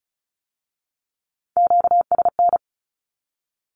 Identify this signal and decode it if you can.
Morse code, heard as a keyed tone, about 36 wpm, 700 Hz. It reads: QSN
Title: Q-Codes with Spoken Meaning - 35wpm